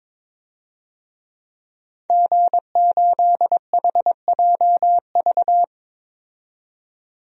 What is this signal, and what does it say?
Morse code, heard as a keyed tone, about 22 wpm, 700 Hz. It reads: G8HJV
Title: DXCC-100 Call Signs - 5 Characters - 22wpm